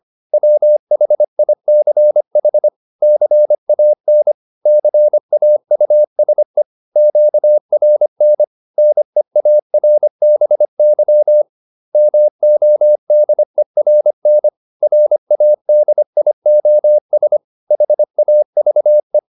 Morse code, heard as a keyed tone, about 25 wpm, 600 Hz. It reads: WHICH CAN CAUSE QRN NEARBY MODERN RADIOS HAVE